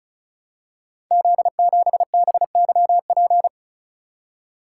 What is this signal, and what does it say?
Morse code, heard as a keyed tone, about 36 wpm, 700 Hz. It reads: Z7BYP
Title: Code Groups - 5 Letter:Numbers - 35wpm